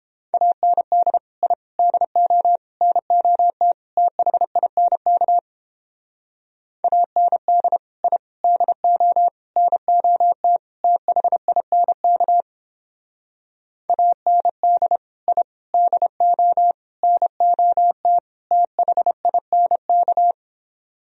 Morse code, heard as a keyed tone, about 31 wpm, 700 Hz. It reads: AND I DO NOT THINK AND I DO NOT THINK AND I DO NOT THINK